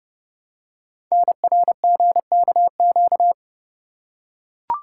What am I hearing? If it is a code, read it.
Morse code, heard as a keyed tone, about 28 wpm, 700 Hz. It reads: NRGKQ
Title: Code Groups - 5 Letters - 30wpm